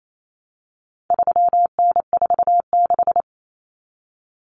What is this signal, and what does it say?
Morse code, heard as a keyed tone, about 28 wpm, 700 Hz. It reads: 3N46
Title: Common Cut Numbers - 4-Digit 28wpm